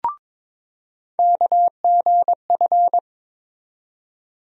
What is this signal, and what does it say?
Morse code, heard as a keyed tone, about 21 wpm, 700 Hz. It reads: KGF
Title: USA Call Sign Postfixes - 3 Letters 22wpm